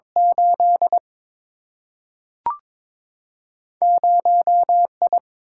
Morse code, heard as a keyed tone, about 22 wpm, 700 Hz. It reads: 8 0 I
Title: Single-Letter Single-Number 22wpm